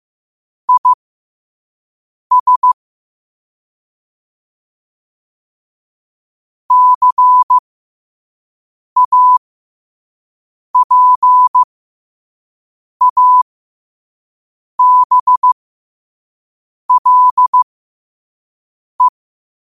Morse code, heard as a keyed tone, about 15 wpm, spaced as for 5 wpm, 1000 Hz. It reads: IS CAPABLE
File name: qotdfiles / 5wpm